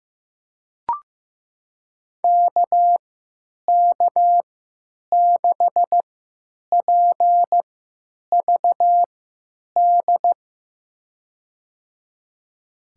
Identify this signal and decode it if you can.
Morse code, heard as a keyed tone, about 15 wpm, spaced as for 8 wpm, 700 Hz. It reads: KK6PVD
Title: USA Call Signs - 6 Characters - 3x Character Spacing - 15wpm